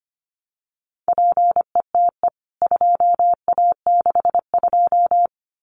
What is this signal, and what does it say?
Morse code, heard as a keyed tone, about 25 wpm, 700 Hz. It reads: PETE 2A62
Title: CWT Exchange 25wpm